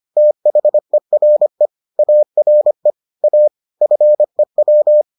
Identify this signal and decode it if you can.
Morse code, heard as a keyed tone, about 25 wpm, 600 Hz. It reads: THERE ARE A FEW